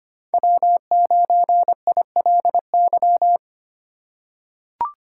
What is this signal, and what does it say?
Morse code, heard as a keyed tone, about 25 wpm, 700 Hz. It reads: W9ILY
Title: K1USN SST QSO Practice - 25wpm